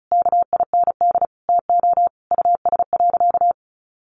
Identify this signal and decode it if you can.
Morse code, heard as a keyed tone, about 34 wpm, 700 Hz. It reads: KIND TO US.
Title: Sentences from Top 700 Words 35wpm